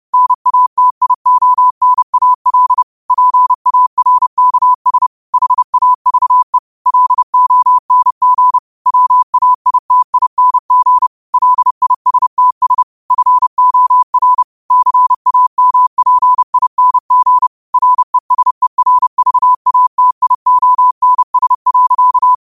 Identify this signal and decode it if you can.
Morse code, heard as a keyed tone, about 30 wpm, 1000 Hz. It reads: NATIONAL PARKS HAVE LONG WAITING LISTS FOR CAMPING RESERVATIONS.
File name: qotdfiles / 30wpm